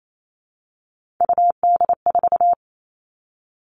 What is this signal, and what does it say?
Morse code, heard as a keyed tone, about 28 wpm, 700 Hz. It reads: UD4
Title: All Cut Numbers - 3-Digit 28wpm